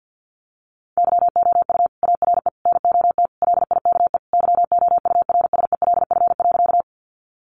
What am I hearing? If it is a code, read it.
Morse code, heard as a keyed tone, about 50 wpm, 700 Hz. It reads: YOU ARE NOT LIKE YOURSELF.
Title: Sentences from Top 600 Words 50wpm